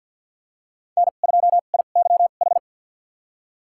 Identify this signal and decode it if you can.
Morse code, heard as a keyed tone, about 47 wpm, spaced as for 35 wpm, 700 Hz. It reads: N2IYH
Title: USA Call Signs - 5 Characters - 2x Character Spacing - 50wpm